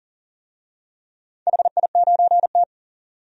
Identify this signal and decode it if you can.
Morse code, heard as a keyed tone, about 38 wpm, 700 Hz. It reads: HI9T
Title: DXCC-100 Call Signs - 4 Characters - 40wpm